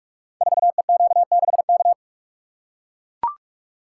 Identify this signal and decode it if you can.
Morse code, heard as a keyed tone, about 44 wpm, 700 Hz. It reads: 4EQ6K
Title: Code Groups - 5 Letter:Numbers - 45wpm